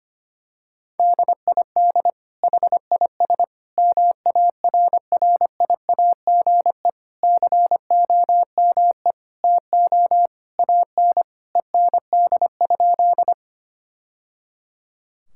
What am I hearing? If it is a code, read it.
Morse code, heard as a keyed tone, about 25 wpm, 700 Hz. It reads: DID HIS MARRIAGE COME TO AN END?
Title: Sentences from Top 1000 Words - Review Part 5 - 25wpm